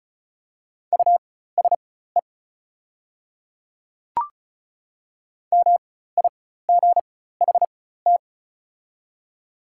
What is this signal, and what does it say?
Morse code, heard as a keyed tone, about 34 wpm, spaced as for 16 wpm, 700 Hz. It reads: USE MIGHT
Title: 4x Character Spacing - Top 300 Words - 35wpm